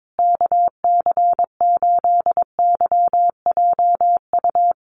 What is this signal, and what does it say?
Morse code, heard as a keyed tone, about 22 wpm, 700 Hz. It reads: KC8YJU
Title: USA Call Signs Repeated 3x 22wpm